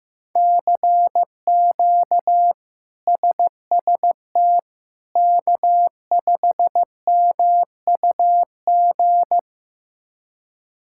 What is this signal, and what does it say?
Morse code, heard as a keyed tone, about 16 wpm, 700 Hz. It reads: CQ SST K5MUG
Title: K1USN SST QSO Practice - 15wpm